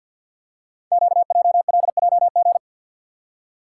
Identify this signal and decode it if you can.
Morse code, heard as a keyed tone, about 50 wpm, 690 Hz. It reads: QJLJG
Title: Code Groups - 5 Letters - 50wpm